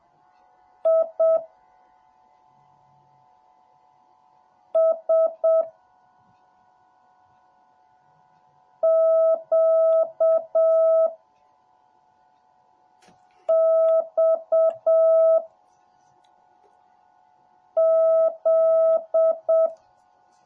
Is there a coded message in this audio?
Yes: Morse code from 0.9 to 19.7 s, heard as a keyed tone, about 7 wpm, 650 Hz. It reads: I S Q X Z